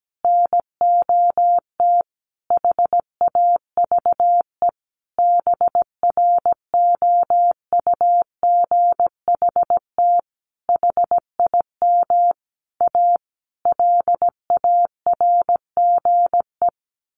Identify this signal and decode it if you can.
Morse code, heard as a keyed tone, about 17 wpm, 700 Hz. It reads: NOT HAVE BROUGHT HIM A LARGE
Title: Sentences from Top 1100 Words 17wpm